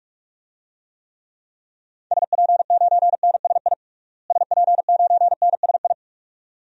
Morse code, heard as a keyed tone, about 46 wpm, 700 Hz. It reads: SP9NSI SP9NSI
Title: DXCC-100 Call Signs - 6 Characters - Repeated 2x - 45wpm